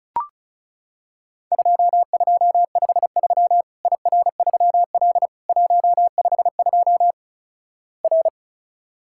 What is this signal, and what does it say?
Morse code, heard as a keyed tone, about 36 wpm, 700 Hz. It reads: 2253 IR3L 152 R
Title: Worked All Europe DX Contest - QTC Exchange - 35wpm